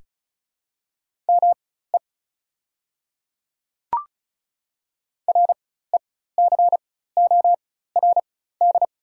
Morse code, heard as a keyed tone, about 34 wpm, spaced as for 16 wpm, 690 Hz. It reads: ME RECORD
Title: 4x Character Spacing - Top 500 Words - 35wpm